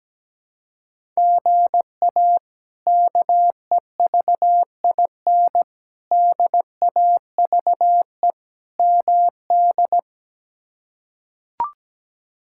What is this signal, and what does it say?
Morse code, heard as a keyed tone, about 17 wpm, 700 Hz. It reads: GA KEVIN DAVE MD
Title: K1USN SST QSO Practice - 17wpm